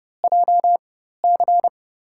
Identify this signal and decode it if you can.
Morse code, heard as a keyed tone, about 31 wpm, spaced as for 14 wpm, 700 Hz. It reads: JC
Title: USA Call Signs - 5 Characters - 4x Character Spacing - 30wpm